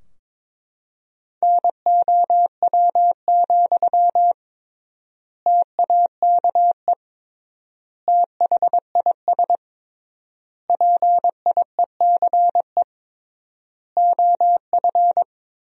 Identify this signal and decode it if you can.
Morse code, heard as a keyed tone, about 22 wpm, 700 Hz. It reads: NOW, TAKE THIS PIECE OF
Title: 3x Word-Spacing - Sentences from Top 400 Words - 22wpm